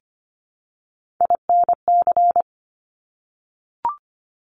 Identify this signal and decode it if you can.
Morse code, heard as a keyed tone, about 25 wpm, 700 Hz. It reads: INC
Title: Top Three Letter Combinations in Words 25wpm